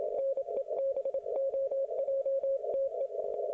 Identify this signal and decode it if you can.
Morse code, heard as a keyed tone, about 27 wpm, 540 Hz. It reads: NEBOJTE